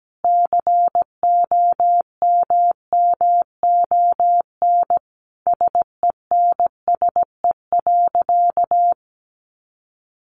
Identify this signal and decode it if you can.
Morse code, heard as a keyed tone, about 17 wpm, 700 Hz. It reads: COMMON SENSE.
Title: Sentences from Top 1300 Words 17wpm